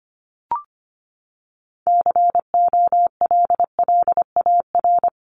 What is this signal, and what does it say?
Morse code, heard as a keyed tone, about 25 wpm, 700 Hz. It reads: COLLAR
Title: Top 1751-2000 Words 25wpm